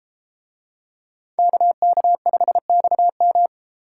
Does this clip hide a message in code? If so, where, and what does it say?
Yes: Morse code from 1.4 to 3.5 s, heard as a keyed tone, about 32 wpm, 700 Hz. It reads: KK5XM